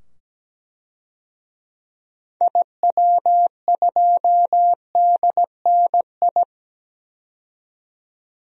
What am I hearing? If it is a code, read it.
Morse code, heard as a keyed tone, about 17 wpm, 700 Hz. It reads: IW2DNI
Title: DXCC-100 Call Signs - 6 Characters - 17wpm